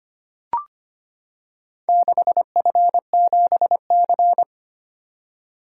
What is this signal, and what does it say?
Morse code, heard as a keyed tone, about 25 wpm, 700 Hz. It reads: 6F7C
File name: Four Letter-Number Combinations - 25wpm